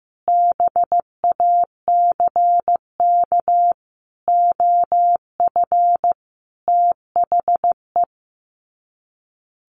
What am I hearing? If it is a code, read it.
Morse code, heard as a keyed tone, about 15 wpm, 700 Hz. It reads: BACK OF THE
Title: Sets of 5 Words Encore 15wpm